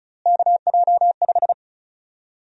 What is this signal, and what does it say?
Morse code, heard as a keyed tone, about 36 wpm, 700 Hz. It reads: KJ5
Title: USA Call Sign Prefixes - Repeated 2x - 35wpm